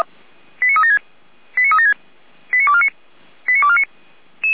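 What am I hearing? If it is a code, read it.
Five-tone selective call (ZVEI), heard as quick runs of five tones, tones of about 70 ms. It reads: 89376 89376 89249 89249
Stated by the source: Received on an SDR, ZVEI five-tone code 89310.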